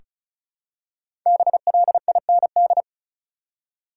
Morse code, heard as a keyed tone, about 36 wpm, 700 Hz. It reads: BLIND
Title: Five Letter Words - 35wpm